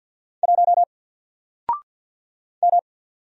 Morse code, heard as a keyed tone, about 48 wpm, 700 Hz. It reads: 1 M
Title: Single Letter-Number Alt 50wpm